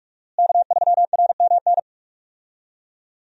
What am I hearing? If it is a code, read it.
Morse code, heard as a keyed tone, about 43 wpm, 700 Hz. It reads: K3RMN
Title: USA Call Signs - 5 Characters - Repeated 3x - 45wpm